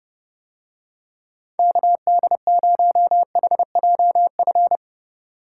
Morse code, heard as a keyed tone, about 27 wpm, 700 Hz. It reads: KD0HJF